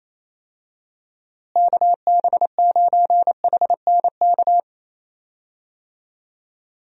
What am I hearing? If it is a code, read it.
Morse code, heard as a keyed tone, about 28 wpm, 700 Hz. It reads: KB9HNK
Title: USA Call Signs - 6 Characters - Repeated 3x - 28wpm